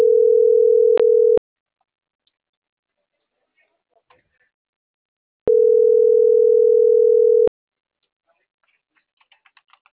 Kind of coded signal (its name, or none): telephone call progress tone